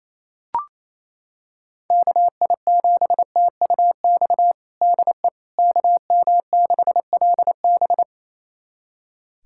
Morse code, heard as a keyed tone, about 28 wpm, 700 Hz. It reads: KI7TUX DE KM6LB
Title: USA Callsign DE Callsign 28wpm